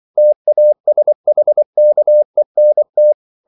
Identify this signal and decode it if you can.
Morse code, heard as a keyed tone, about 24 wpm, 600 Hz. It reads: TASHKENT
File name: CW / Practice / NationalCapitals24